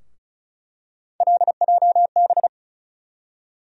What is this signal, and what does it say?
Morse code, heard as a keyed tone, about 36 wpm, 700 Hz. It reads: LJB
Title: USA Call Sign Postfixes - 3 Letters 35wpm